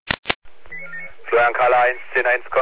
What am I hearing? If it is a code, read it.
Five-tone selective call (ZVEI), heard as quick runs of five tones, tones of about 70 ms, with speech. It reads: 89479